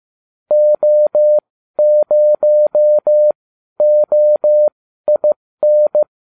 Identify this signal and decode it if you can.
Morse code, heard as a keyed tone, about 15 wpm, spaced as for 12 wpm, 600 Hz. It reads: O0OIN